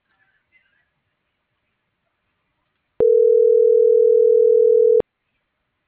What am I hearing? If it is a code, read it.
Ringback tone (North America), 440 + 480 Hz.